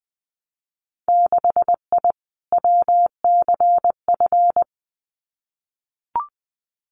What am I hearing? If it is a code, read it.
Morse code, heard as a keyed tone, about 20 wpm, 700 Hz. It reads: 6I WCF
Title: Winter Field Day Exchanges Encore 20wpm